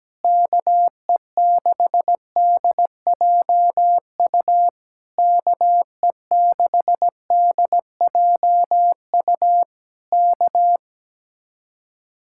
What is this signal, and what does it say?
Morse code, heard as a keyed tone, about 17 wpm, 700 Hz. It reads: KE6DJU KE6DJU K
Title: Speed-Racing Common QSO Phrases - 17wpm